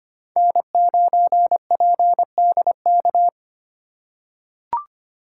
Morse code, heard as a keyed tone, about 25 wpm, 700 Hz. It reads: N9PDK
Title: USA Call Signs - 5 Characters - 2x Character Spacing - 25wpm